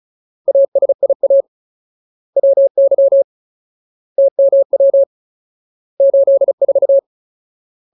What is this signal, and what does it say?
Morse code, heard as a keyed tone, about 34 wpm, 550 Hz. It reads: ASIA WY TMW 84
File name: Ultimate Word List / 35WPM Ultimate